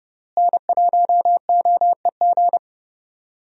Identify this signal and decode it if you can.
Morse code, heard as a keyed tone, about 30 wpm, 700 Hz. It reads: N1OEG
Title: USA Call Signs - 5 Characters - Repeated 2x - 30wpm